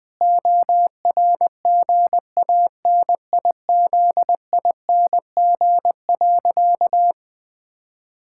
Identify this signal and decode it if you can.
Morse code, heard as a keyed tone, about 20 wpm, 700 Hz. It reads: ORGANIZING.
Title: Sentences with 7-Word Phrases - 20wpm